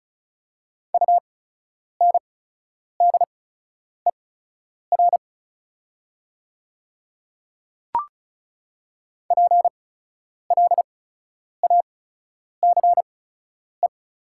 Morse code, heard as a keyed tone, about 36 wpm, spaced as for 10 wpm, 690 Hz. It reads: UNDER PLACE
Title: 8x Character Spacing - Top 200 Words - 35wpm